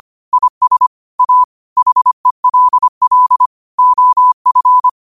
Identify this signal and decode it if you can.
Morse code, heard as a keyed tone, about 25 wpm, 1000 Hz. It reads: IS A HELL OF
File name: qotdfiles / 25wpm